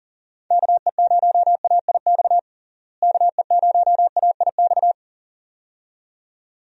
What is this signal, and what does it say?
Morse code, heard as a keyed tone, about 40 wpm, 700 Hz. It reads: KE0AIX KE0AIX
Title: USA Call Signs - 6 Characters - Repeated 2x - 40wpm